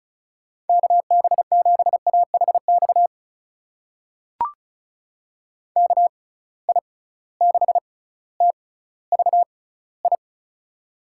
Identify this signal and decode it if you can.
Morse code, heard as a keyed tone, about 34 wpm, spaced as for 18 wpm, 700 Hz. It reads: KB7AHX KI6TVI
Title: USA Call Signs - 6 Characters - 6x Character Spacing - 35wpm